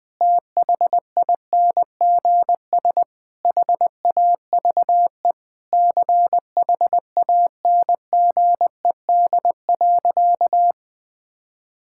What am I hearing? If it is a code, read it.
Morse code, heard as a keyed tone, about 20 wpm, 700 Hz. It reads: THINGS HAVE CHANGED.
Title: Sentences with 5-Word Phrases - 20wpm